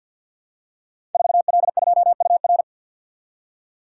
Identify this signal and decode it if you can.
Morse code, heard as a keyed tone, about 48 wpm, 700 Hz. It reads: 4L2UR
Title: Code Groups - 5 Letter:Numbers - 50wpm